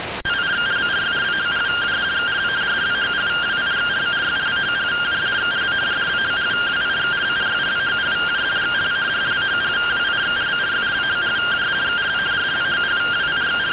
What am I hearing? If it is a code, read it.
Two-tone teleprinter signal, about 170 Hz shift.